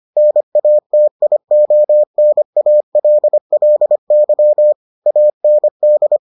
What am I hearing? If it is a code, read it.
Morse code, heard as a keyed tone, about 25 wpm, 600 Hz. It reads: NATIONALLY AND